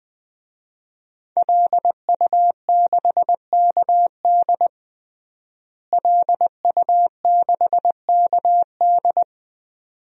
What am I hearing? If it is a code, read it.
Morse code, heard as a keyed tone, about 21 wpm, 700 Hz. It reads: LU6KD LU6KD
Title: DXCC-100 Call Signs - 5 Characters - Repeated 2x - 20wpm